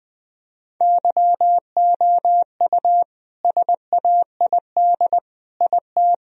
Morse code, heard as a keyed tone, about 21 wpm, 700 Hz. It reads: YOU SAID IT
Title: Sentences from Top 1000 Words - Review Part 2 - 20wpm